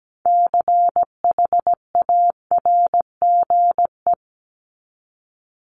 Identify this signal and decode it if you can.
Morse code, heard as a keyed tone, about 17 wpm, 700 Hz. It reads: CHARGE
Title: Top 751-1000 Words 17wpm